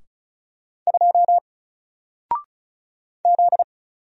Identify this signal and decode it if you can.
Morse code, heard as a keyed tone, about 34 wpm, 700 Hz. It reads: 2 Z
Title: Single Letter-Number Alt 35wpm